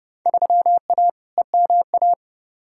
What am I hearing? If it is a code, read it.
Morse code, heard as a keyed tone, about 30 wpm, 700 Hz. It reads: 3A EMA
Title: Field Day Exchanges - 30wpm